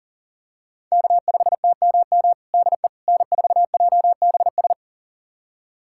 Morse code, heard as a keyed tone, about 40 wpm, 700 Hz. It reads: K5TMM DE N4JBS
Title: USA Callsign DE Callsign - 5 Characters - 40wpm